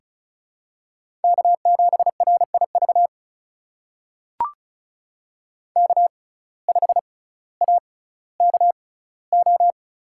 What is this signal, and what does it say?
Morse code, heard as a keyed tone, about 36 wpm, spaced as for 18 wpm, 700 Hz. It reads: K7RIV K5AKO
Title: USA Call Signs - 5 Characters - 6x Character Spacing - 35wpm